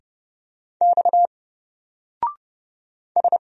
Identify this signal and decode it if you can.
Morse code, heard as a keyed tone, about 30 wpm, 700 Hz. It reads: X S
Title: Single Character (Letter:Number:Punctuation:Prosign) Alt 30wpm